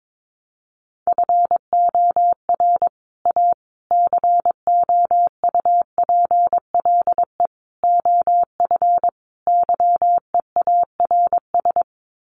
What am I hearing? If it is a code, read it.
Morse code, heard as a keyed tone, about 22 wpm, 700 Hz. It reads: FOR A COUPLE OF YEARS
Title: Sets of 5 Words Encore 22wpm